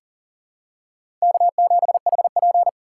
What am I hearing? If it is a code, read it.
Morse code, heard as a keyed tone, about 40 wpm, 700 Hz. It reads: K7HP
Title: USA Call Signs - 4 Characters - 4x Character Spacing - 40wpm